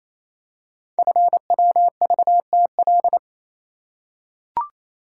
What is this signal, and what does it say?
Morse code, heard as a keyed tone, about 28 wpm, 700 Hz. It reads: FWVTL
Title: Code Groups - 5 Letter:Numbers - 28wpm